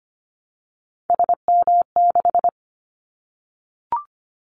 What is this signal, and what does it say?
Morse code, heard as a keyed tone, about 25 wpm, 700 Hz. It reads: SM6
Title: Callsign Prefixes for Top 25 DXCC - Repeated 3x - 25wpm